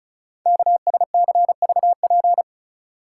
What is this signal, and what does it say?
Morse code, heard as a keyed tone, about 36 wpm, 690 Hz. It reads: KSCVP
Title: Code Groups - 5 Letters - 35wpm